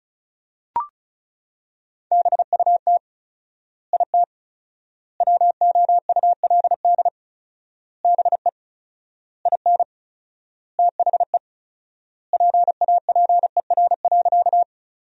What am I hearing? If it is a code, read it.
Morse code, heard as a keyed tone, about 34 wpm, 700 Hz. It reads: BUT IT WOULD BE IN THE PAPER.
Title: 4x Word-Spacing - Sentences from Top 300 Words - 35wpm